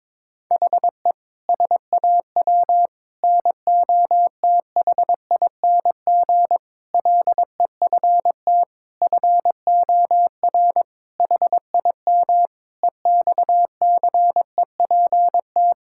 Morse code, heard as a keyed tone, about 22 wpm, 700 Hz. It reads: HE SAW NOTHING LEFT FOR HIM EXCEPT
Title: Sentences from Top 1600 Words 22wpm